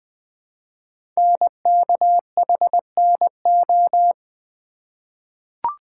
Morse code, heard as a keyed tone, about 20 wpm, 700 Hz. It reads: NKHNO
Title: Code Groups - 5 Letters - 20wpm